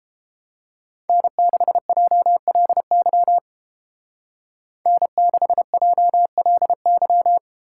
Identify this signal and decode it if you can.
Morse code, heard as a keyed tone, about 32 wpm, 700 Hz. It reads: N6JLY N6JLY